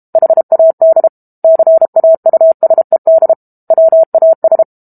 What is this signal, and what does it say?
Morse code, heard as a keyed tone, about 32 wpm, 650 Hz. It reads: HAD CAUSED WAS
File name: Book Two / 06-Chapter 6 - 32 WPM